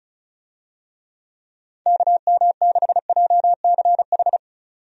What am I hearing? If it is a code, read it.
Morse code, heard as a keyed tone, about 36 wpm, 700 Hz. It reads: KM6JCH
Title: USA Call Signs - 6 Characters - 35wpm